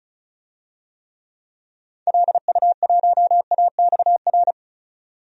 Morse code, heard as a keyed tone, about 36 wpm, 700 Hz. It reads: LU1AXR
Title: DXCC-100 Call Signs - 6 Characters - 35wpm